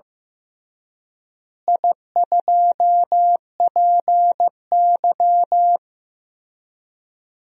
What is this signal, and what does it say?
Morse code, heard as a keyed tone, about 15 wpm, 700 Hz. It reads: I2PY
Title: DXCC-100 Call Signs - 4 Characters - Repeated 3x - 15wpm